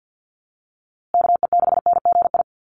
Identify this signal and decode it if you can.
Morse code, heard as a keyed tone, about 50 wpm, 700 Hz. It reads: KE6NGI
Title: USA Call Signs Repeated 2x 50wpm